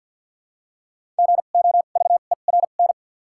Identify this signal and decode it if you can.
Morse code, heard as a keyed tone, about 53 wpm, spaced as for 37 wpm, 700 Hz. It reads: GOVERN
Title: 2x Character Spacing - Top 500 Words - 50wpm